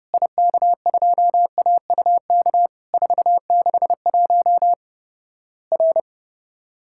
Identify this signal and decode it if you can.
Morse code, heard as a keyed tone, about 28 wpm, 700 Hz. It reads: IK2AUK 461 R
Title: Worked All Europe DX Contest - QTC Exchange - 30wpm